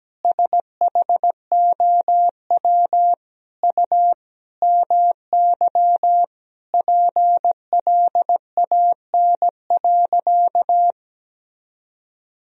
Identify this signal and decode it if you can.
Morse code, heard as a keyed tone, about 17 wpm, 700 Hz. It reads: SHOW U MY PLAN.
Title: Sentences from Top 500 Words and Abbreviations 17wpm